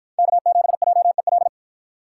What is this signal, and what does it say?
Morse code, heard as a keyed tone, about 50 wpm, 700 Hz. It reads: K7JEL